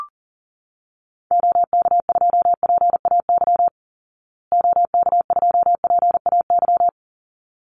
Morse code, heard as a keyed tone, about 40 wpm, 700 Hz. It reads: OK2PAY OK2PAY
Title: Call Signs Repeated 3x - 40wpm